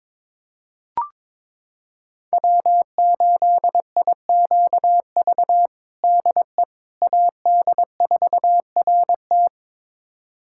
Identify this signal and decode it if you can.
Morse code, heard as a keyed tone, about 22 wpm, 700 Hz. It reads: W8IQV DE AD4RT
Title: USA Callsign DE Callsign 22wpm